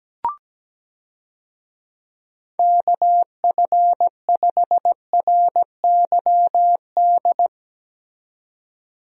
Morse code, heard as a keyed tone, about 17 wpm, 700 Hz. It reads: KF5RYD
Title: USA Call Signs - 6 Characters - 17wpm